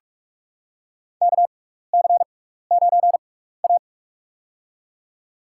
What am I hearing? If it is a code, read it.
Morse code, heard as a keyed tone, about 46 wpm, spaced as for 16 wpm, 700 Hz. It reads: KC9A
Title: USA Call Signs - 4 Characters - 6x Character Spacing - 45wpm